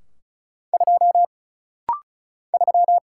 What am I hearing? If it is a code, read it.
Morse code, heard as a keyed tone, about 34 wpm, 700 Hz. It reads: 2 3
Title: Single Letter-Number - Rapid-Fire 35wpm